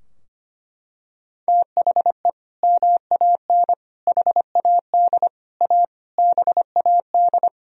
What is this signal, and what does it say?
Morse code, heard as a keyed tone, about 25 wpm, 700 Hz. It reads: THE MAN HAD A BAD
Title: Sentences from Top 1000 Words - Review Part 2 - 25wpm